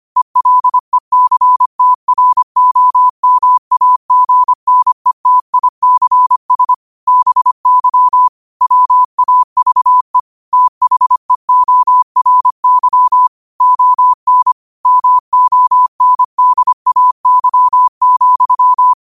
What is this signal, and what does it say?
Morse code, heard as a keyed tone, about 25 wpm, 1000 Hz. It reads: ELECTROMAGNETICS BY WAVE THEORY ON MONDAY,
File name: qotdfiles / 25wpm